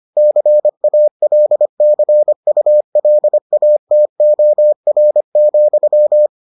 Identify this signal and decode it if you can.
Morse code, heard as a keyed tone, about 25 wpm, 600 Hz. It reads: CALCULATOR,